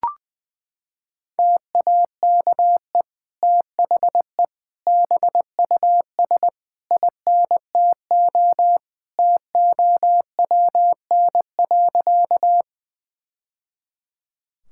Morse code, heard as a keyed tone, about 21 wpm, 700 Hz. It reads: TAKE THE BUS INTO TOWN.
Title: Sentences from Top 1000 Words - Review Part 6 - 20wpm